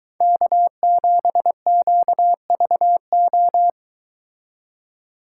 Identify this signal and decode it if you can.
Morse code, heard as a keyed tone, about 22 wpm, 700 Hz. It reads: K7QVO